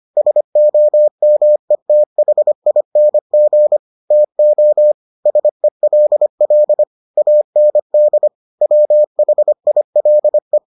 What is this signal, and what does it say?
Morse code, heard as a keyed tone, about 25 wpm, 600 Hz. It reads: SOMETHING TO SELL AND WHILE